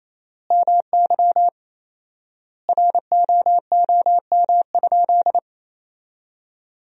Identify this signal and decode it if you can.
Morse code, heard as a keyed tone, about 28 wpm, 700 Hz. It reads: MY ROOM?
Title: 4x Word-Spacing - Sentences from Top 300 Words - 28wpm